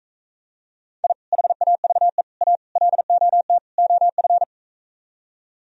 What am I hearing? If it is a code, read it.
Morse code, heard as a keyed tone, about 44 wpm, 700 Hz. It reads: I HAVE A LOT OF